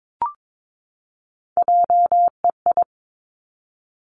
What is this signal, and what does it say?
Morse code, heard as a keyed tone, about 22 wpm, 700 Hz. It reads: JEI
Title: Three Challenging Letter-Number Combinations 22wpm